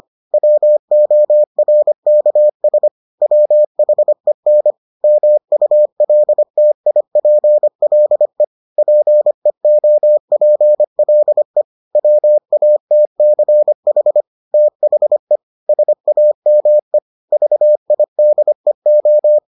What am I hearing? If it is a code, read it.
Morse code, heard as a keyed tone, about 25 wpm, 600 Hz. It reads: WORKS WHEN MULTIPLE PEOPLE WATCH THE SAME VIDEO